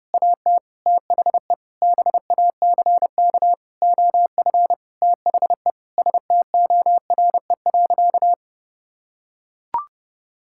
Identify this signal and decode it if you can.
Morse code, heard as a keyed tone, about 30 wpm, 700 Hz. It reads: AT THE BACK OF THE STORE.
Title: Sentences from Top 1000 Words - Review Part 5 - 30wpm